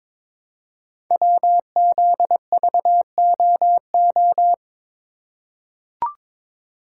Morse code, heard as a keyed tone, about 22 wpm, 700 Hz. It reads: WZVOO
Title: Code Groups - 5 Letter:Numbers - 22wpm